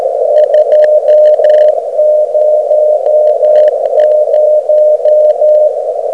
Morse code, heard as a keyed tone, about 13 wpm, 610 Hz, with radio noise. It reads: 200
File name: Beacon recordings / OK0EF